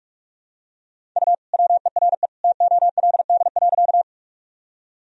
Morse code, heard as a keyed tone, about 46 wpm, 700 Hz. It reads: U WERE TOLD.
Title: Sentences from Top 400 Words and Abbreviations 45wpm